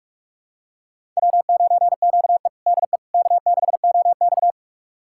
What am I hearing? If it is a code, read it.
Morse code, heard as a keyed tone, about 46 wpm, 700 Hz. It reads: W9QE DE K6OX
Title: USA Callsign DE Callsign - 4 Characters - 45wpm